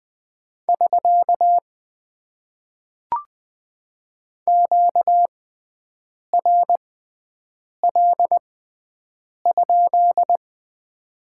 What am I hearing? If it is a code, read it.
Morse code, heard as a keyed tone, about 20 wpm, spaced as for 7 wpm, 700 Hz. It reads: <SK> QRL?
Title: Top 90 QSO Elements - 6x Character Spacing - 20wpm